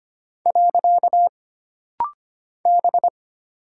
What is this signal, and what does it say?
Morse code, heard as a keyed tone, about 25 wpm, 700 Hz. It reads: . B
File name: Single Character (Letter:Number:Punctuation:Prosign) - Rapid-Fire - 25wpm